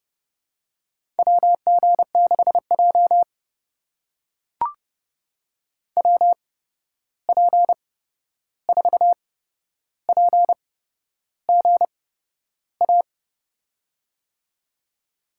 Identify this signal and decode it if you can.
Morse code, heard as a keyed tone, about 30 wpm, spaced as for 11 wpm, 700 Hz. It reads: WG6J WP4PGA
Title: USA Call Signs - 8x Character Spacing - 30wpm